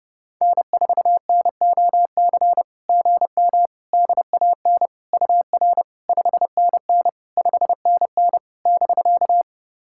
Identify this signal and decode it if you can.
Morse code, heard as a keyed tone, about 30 wpm, 690 Hz. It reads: N4NOC GM DAN UR 5NN 5NN <BK>
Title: SOTA QSO Practice 30wpm